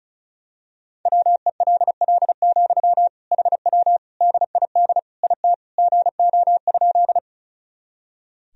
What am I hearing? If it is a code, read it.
Morse code, heard as a keyed tone, about 34 wpm, 700 Hz. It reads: WELL, HW DID IT GO?
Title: Sentences from Top 200 Words and Abbreviations 35wpm